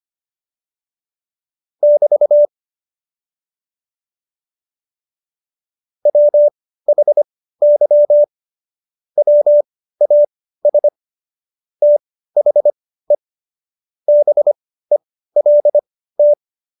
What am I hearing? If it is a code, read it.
Morse code, heard as a keyed tone, about 25 wpm, spaced as for 15 wpm, 590 Hz. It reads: <BT> WHY WAS THE BELT
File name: Jokes-25-char-15-eff